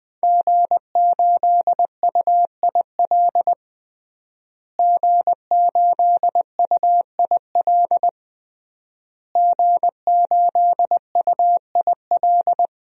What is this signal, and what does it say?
Morse code, heard as a keyed tone, about 20 wpm, 700 Hz. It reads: G8UIL G8UIL G8UIL
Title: DXCC-100 Call Signs - 5 Characters - Repeated 3x - 20wpm